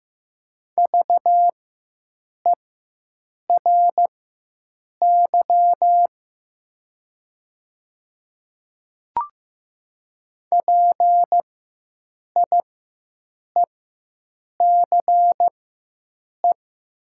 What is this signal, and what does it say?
Morse code, heard as a keyed tone, about 15 wpm, spaced as for 7 wpm, 700 Hz. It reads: VERY PIECE
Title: 4x Character Spacing - Top 500 Words - 15wpm